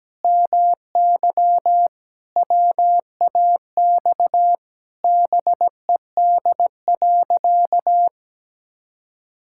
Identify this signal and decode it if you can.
Morse code, heard as a keyed tone, about 17 wpm, 700 Hz. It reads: MY WAX BED.
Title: Pangram Challenge v2 - 17wpm